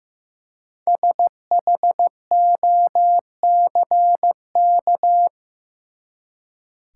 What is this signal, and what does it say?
Morse code, heard as a keyed tone, about 15 wpm, 700 Hz. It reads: SHOCK
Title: Five Letter Words - 15wpm